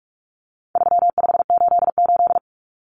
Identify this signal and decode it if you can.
Morse code, heard as a keyed tone, about 44 wpm, 700 Hz. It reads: 3588
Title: Common Cut Numbers - 4-Digit 45wpm